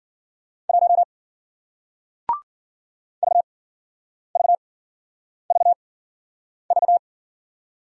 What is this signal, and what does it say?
Morse code, heard as a keyed tone, about 55 wpm, spaced as for 9 wpm, 700 Hz. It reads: 1 VVVV